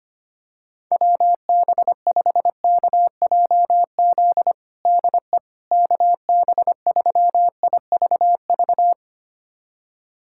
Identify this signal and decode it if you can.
Morse code, heard as a keyed tone, about 25 wpm, 700 Hz. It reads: WB5KJZ DE KB3IVV
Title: USA Callsign DE Callsign - 6 Characters - 25wpm